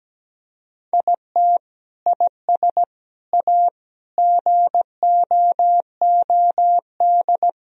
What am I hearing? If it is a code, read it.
Morse code, heard as a keyed tone, about 17 wpm, 700 Hz. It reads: IT IS A GOOD
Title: Sentences from Top 1000 Words - Review Part 3 - 17wpm